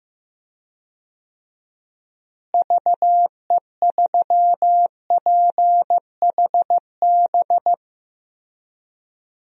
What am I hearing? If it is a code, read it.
Morse code, heard as a keyed tone, about 15 wpm, 700 Hz. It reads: VE3PHB
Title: DXCC-100 Call Signs - 6 Characters - Repeated 3x - 15wpm